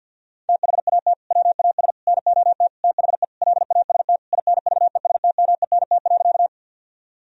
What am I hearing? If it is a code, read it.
Morse code, heard as a keyed tone, about 50 wpm, 700 Hz. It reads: THAT WAS NOT THE LAST INVESTMENT.
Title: Sentences from Top 1000 Words - Review Part 4 - 50wpm